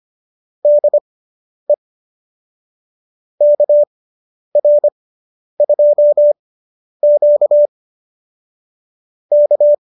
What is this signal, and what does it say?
Morse code, heard as a keyed tone, about 25 wpm, spaced as for 10 wpm, 590 Hz. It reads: DE KR2Q K